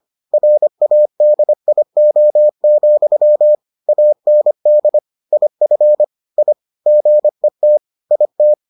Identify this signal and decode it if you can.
Morse code, heard as a keyed tone, about 25 wpm, 600 Hz. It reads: RADIO, AND IF I GET IT